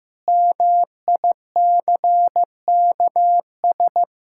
Morse code, heard as a keyed tone, about 15 wpm, 700 Hz. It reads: MICKS
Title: Pangram Challenge v2 - 15wpm